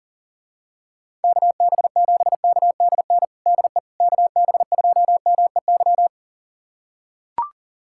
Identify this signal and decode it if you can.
Morse code, heard as a keyed tone, about 40 wpm, 700 Hz. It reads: KB7KDN DE KB2MEY
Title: USA Callsign DE Callsign - 6 Characters - 40wpm